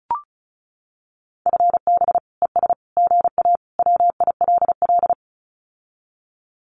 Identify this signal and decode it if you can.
Morse code, heard as a keyed tone, about 34 wpm, 700 Hz. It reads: FB ES GA WILL
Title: Common QSO Phrases Encore 35wpm